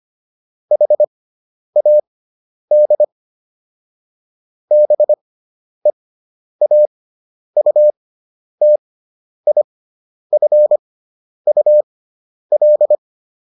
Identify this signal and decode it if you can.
Morse code, heard as a keyed tone, about 25 wpm, spaced as for 10 wpm, 600 Hz. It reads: HAD BEAUTIFUL